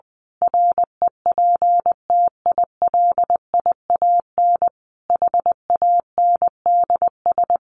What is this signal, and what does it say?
Morse code, heard as a keyed tone, about 20 wpm, 700 Hz. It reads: REPTILIAN HANDS
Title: A Sound Of Thunder 20wpm